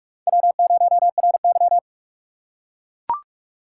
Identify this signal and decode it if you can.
Morse code, heard as a keyed tone, about 43 wpm, 690 Hz. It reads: W0RY
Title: USA Call Signs - 4 Characters - 2x Character Spacing - 45wpm